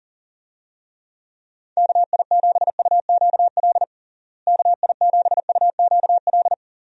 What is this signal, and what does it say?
Morse code, heard as a keyed tone, about 40 wpm, 700 Hz. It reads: KI7UQL KI7UQL
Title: USA Call Signs - 6 Characters - Repeated 2x - 40wpm